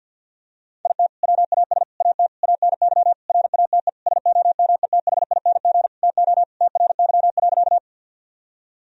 Morse code, heard as a keyed tone, about 50 wpm, 700 Hz. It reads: IT WAS AT ANY RATE SOMETHING TO TRY.
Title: Sentences from Top 1000 Words - Review Part 4 - 50wpm